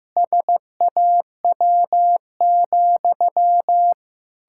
Morse code, heard as a keyed tone, about 15 wpm, 700 Hz. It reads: SAW,
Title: Pangram Challenge v2 - 15wpm